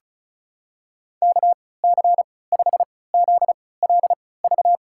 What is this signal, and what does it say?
Morse code, heard as a keyed tone, about 36 wpm, spaced as for 20 wpm, 700 Hz. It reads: KC5ZLV
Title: USA Call Signs - 6 Characters - 3x Character Spacing - 35wpm